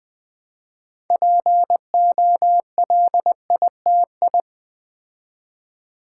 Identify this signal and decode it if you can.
Morse code, heard as a keyed tone, about 20 wpm, 690 Hz. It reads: POLITI
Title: Build A Word - 3 Letters At A Time - 9-Letter Words - 20wpm